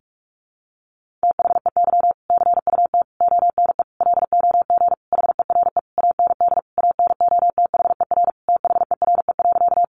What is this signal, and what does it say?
Morse code, heard as a keyed tone, about 44 wpm, 700 Hz. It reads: THEY CUT ONE LOG HERE AND ANOTHER THERE.
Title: Sentences from Top 1400 Words 45wpm